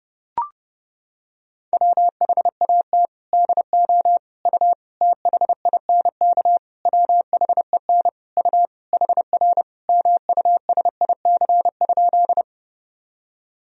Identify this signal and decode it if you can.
Morse code, heard as a keyed tone, about 30 wpm, 700 Hz. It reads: WHAT DO U THINK WHEN U HR MUSIC?
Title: Sentences from Top 300 Words and Abbreviations 30wpm